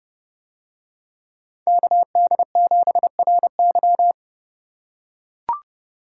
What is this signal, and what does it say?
Morse code, heard as a keyed tone, about 29 wpm, 700 Hz. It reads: KD7RY